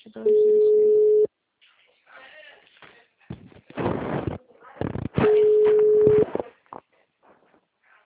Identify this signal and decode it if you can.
Ringback tone (Europe, 425 Hz), with speech.
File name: out- -2020- - - .9767